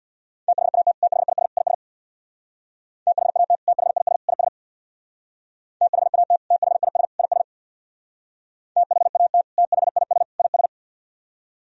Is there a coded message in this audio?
Yes: Morse code from 0.5 to 10.7 s, heard as a keyed tone, about 64 wpm, 700 Hz. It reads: THAT THIS IS THAT THIS IS THAT THIS IS THAT THIS IS